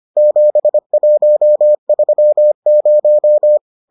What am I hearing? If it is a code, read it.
Morse code, heard as a keyed tone, about 25 wpm, 600 Hz. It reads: 7130